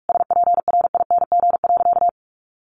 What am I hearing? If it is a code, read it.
Morse code, heard as a keyed tone, about 44 wpm, 700 Hz. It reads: SPRING.
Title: Sentences from Top 1200 Words 45wpm